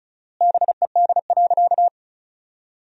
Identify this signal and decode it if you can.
Morse code, heard as a keyed tone, about 36 wpm, 700 Hz. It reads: BED.
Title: 4x Word-Spacing - Sentences from Top 500 Words - 35wpm